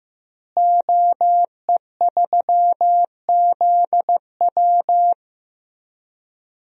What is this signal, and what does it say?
Morse code, heard as a keyed tone, about 15 wpm, 700 Hz. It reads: OE3ZW
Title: DXCC-100 Call Signs - 5 Characters - Repeated 3x - 15wpm